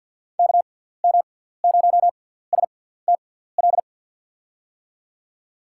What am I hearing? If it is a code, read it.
Morse code, heard as a keyed tone, about 48 wpm, spaced as for 17 wpm, 700 Hz. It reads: KM0STL
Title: USA Call Signs - 6 Characters - 6x Character Spacing - 50wpm